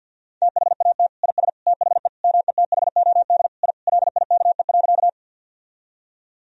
Morse code, heard as a keyed tone, about 51 wpm, 700 Hz. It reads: THAT IS THE METHOD I LIKE.
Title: Sentences from Top 1000 Words - Review Part 3 - 50wpm